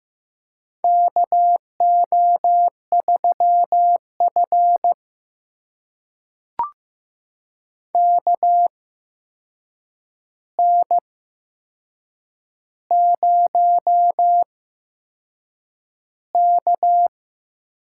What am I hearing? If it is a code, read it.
Morse code, heard as a keyed tone, about 15 wpm, spaced as for 6 wpm, 700 Hz. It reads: KO3F KN0K
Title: USA Call Signs - 4 Characters - 8x Character Spacing - 15wpm